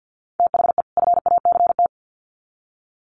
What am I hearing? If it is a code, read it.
Morse code, heard as a keyed tone, about 50 wpm, 700 Hz. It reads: THE FACT
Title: Sets of 2 Words 50wpm